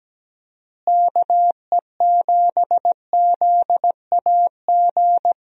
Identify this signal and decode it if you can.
Morse code, heard as a keyed tone, about 17 wpm, 700 Hz. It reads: KE7ZAG
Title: USA Call Signs - 6 Characters - Repeated 3x - 17wpm